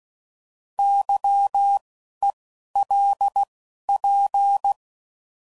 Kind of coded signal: Morse code